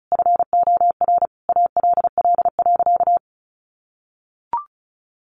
Morse code, heard as a keyed tone, about 34 wpm, 700 Hz. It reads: FOR ALL.
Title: Sentences from Top 1500 Words 35wpm